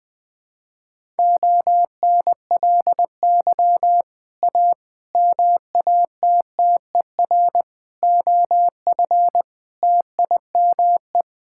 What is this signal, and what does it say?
Morse code, heard as a keyed tone, about 20 wpm, 700 Hz. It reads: ONLY A MATTER OF TIME
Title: Sets of 6 Words - 20wpm